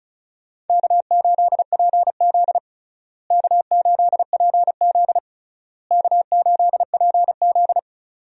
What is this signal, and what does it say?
Morse code, heard as a keyed tone, about 36 wpm, 700 Hz. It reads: K8PZ K8PZ K8PZ
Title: USA Call Signs - 4 Characters - Repeated 3x - 35wpm